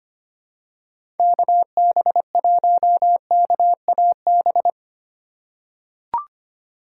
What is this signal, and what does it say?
Morse code, heard as a keyed tone, about 25 wpm, 700 Hz. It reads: KB1KAB
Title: USA Call Signs - 6 Characters - 3x Character Spacing - 25wpm